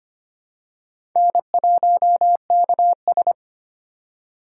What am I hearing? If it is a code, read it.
Morse code, heard as a keyed tone, about 25 wpm, 700 Hz. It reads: N1KS
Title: USA Call Signs - 4 Characters - 3x Character Spacing - 25wpm